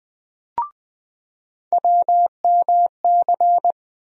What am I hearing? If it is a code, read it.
Morse code, heard as a keyed tone, about 20 wpm, 700 Hz. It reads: WMC
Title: USA Call Sign Postfixes - 3 Letters 20wpm